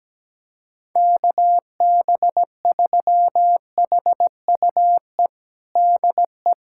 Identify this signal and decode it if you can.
Morse code, heard as a keyed tone, about 17 wpm, 700 Hz. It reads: KB3HUE DE
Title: USA Callsign DE Callsign - 6 Characters - 17wpm